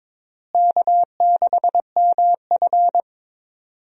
Morse code, heard as a keyed tone, about 22 wpm, 700 Hz. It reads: K6MF
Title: USA Call Signs - 4 Characters - Repeated 3x - 22wpm